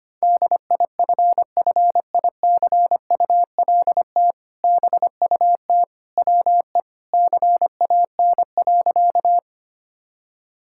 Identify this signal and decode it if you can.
Morse code, heard as a keyed tone, about 25 wpm, 700 Hz. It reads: DIFFICULT BUT WE CAN.
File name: practice / 25wpm / Sentences with 2-Word Phrases - 25wpm